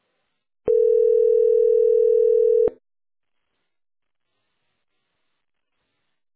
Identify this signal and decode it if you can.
Ringback tone (North America), 440 + 480 Hz.